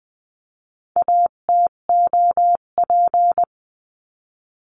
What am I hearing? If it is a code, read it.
Morse code, heard as a keyed tone, about 20 wpm, 700 Hz. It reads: ATOP
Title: 258 Lesson 42c - Words 16-20wpm